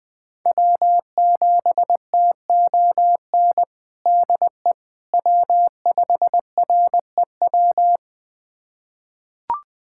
Morse code, heard as a keyed tone, about 20 wpm, 700 Hz. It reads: W7TON DE W5REW
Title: USA Callsign DE Callsign - 5 Characters - 20wpm